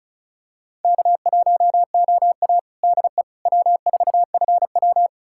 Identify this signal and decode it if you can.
Morse code, heard as a keyed tone, about 34 wpm, 700 Hz. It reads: K1OA DE W4FW
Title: USA Callsign DE Callsign - 4 Characters - 35wpm